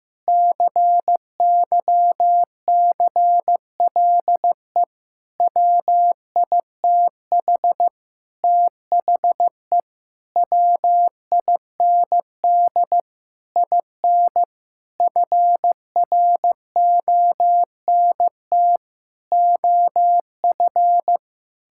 Morse code, heard as a keyed tone, about 15 wpm, 700 Hz. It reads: CYCLE WITH THE WIND IN FRONT OF
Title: Sentences from Top 1000 Words - Review Part 6 - 15wpm